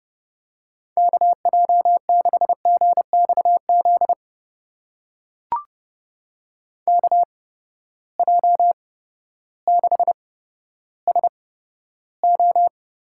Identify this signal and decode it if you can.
Morse code, heard as a keyed tone, about 31 wpm, spaced as for 14 wpm, 700 Hz. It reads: KJ6GXZ KJ6SO
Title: USA Call Signs - 6 Characters - 8x Character Spacing - 30wpm